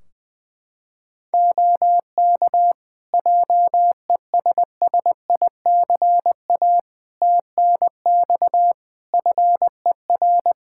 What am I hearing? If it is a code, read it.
Morse code, heard as a keyed tone, about 20 wpm, 700 Hz. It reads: OK JESSICA TNX FER
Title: Practice QSO - Exchange 4 - Ending - 20wpm